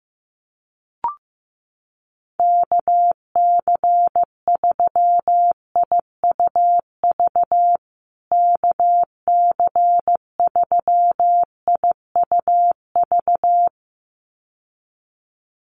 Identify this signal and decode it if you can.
Morse code, heard as a keyed tone, about 14 wpm, 700 Hz. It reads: KC3IUV KC3IUV
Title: USA Call Signs Repeated 2x 15wpm